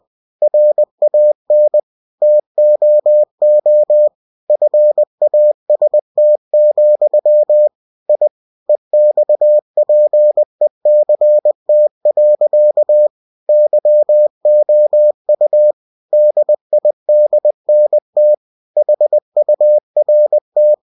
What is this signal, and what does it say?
Morse code, heard as a keyed tone, about 20 wpm, 600 Hz. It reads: RAN TOO FAST, I EXPECT. YOU DIDNT HURT
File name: cw-books / winnie-the-pooh / winnie-20wpm-page-0055